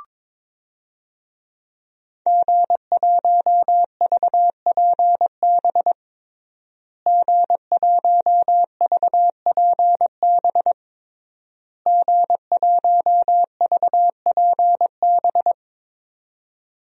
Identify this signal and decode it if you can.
Morse code, heard as a keyed tone, about 22 wpm, 700 Hz. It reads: G1VPB G1VPB G1VPB
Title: DXCC-100 Call Signs - 5 Characters - Repeated 3x - 22wpm